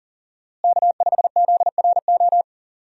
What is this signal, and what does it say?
Morse code, heard as a keyed tone, about 40 wpm, 700 Hz. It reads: K5ZRO
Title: USA Call Signs - 5 Characters - Repeated 3x - 40wpm